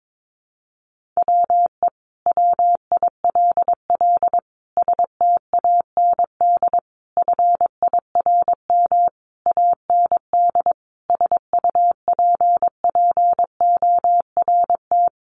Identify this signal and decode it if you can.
Morse code, heard as a keyed tone, about 22 wpm, 700 Hz. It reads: WE WILL STAND FIRM AND SUPPORT
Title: Sentences from Top 1400 Words 22wpm